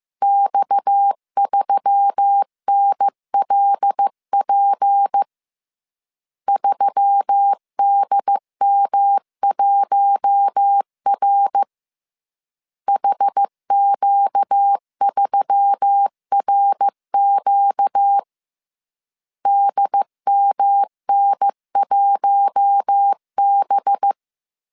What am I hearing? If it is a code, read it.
Morse code, heard as a keyed tone, about 15 wpm, 800 Hz. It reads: X3NLP 3DM1R HQ3RQ DMN1B